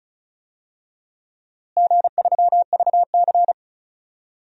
Morse code, heard as a keyed tone, about 36 wpm, 700 Hz. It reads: G3VC
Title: DXCC-100 Call Signs - 4 Characters - Repeated 3x - 35wpm